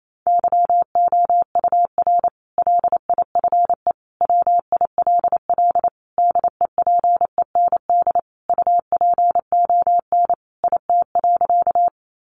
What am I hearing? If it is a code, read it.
Morse code, heard as a keyed tone, about 28 wpm, 700 Hz. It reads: YOUR LIFE WILL DEPEND UPON IT.
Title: Sentences from Top 1300 Words 28wpm